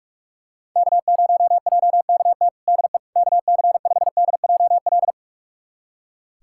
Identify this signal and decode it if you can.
Morse code, heard as a keyed tone, about 42 wpm, 690 Hz. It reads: K0JKT DE KC5DJL
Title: USA Callsign DE Callsign 45wpm